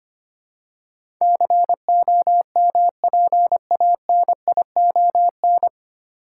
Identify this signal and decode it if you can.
Morse code, heard as a keyed tone, about 25 wpm, 700 Hz. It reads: COMPANION
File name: Nine Letter Words - 25wpm